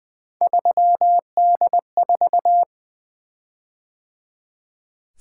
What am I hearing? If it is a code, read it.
Morse code, heard as a keyed tone, about 20 wpm, 700 Hz. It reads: 3D4
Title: Three Letter-Number Combinations - 20wpm